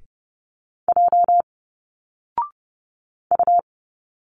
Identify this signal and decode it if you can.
Morse code, heard as a keyed tone, about 30 wpm, 700 Hz. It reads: J U
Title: Single Letters 30wpm